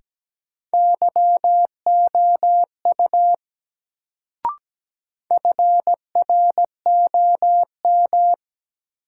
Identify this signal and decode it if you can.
Morse code, heard as a keyed tone, about 17 wpm, 700 Hz. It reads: YOU FROM
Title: Sets of 2 Words - New Word Drill - 17wpm